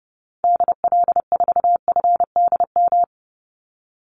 Morse code, heard as a keyed tone, about 30 wpm, 700 Hz. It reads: DL4FDM
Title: Call Signs Repeated 2x - 30wpm